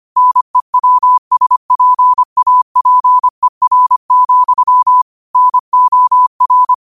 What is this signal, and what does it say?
Morse code, heard as a keyed tone, about 25 wpm, 1000 Hz. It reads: NEWSPAPER, NOR